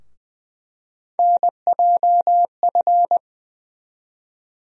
Morse code, heard as a keyed tone, about 20 wpm, 700 Hz. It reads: NJF
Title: USA Call Sign Postfixes - 3 Letters 20wpm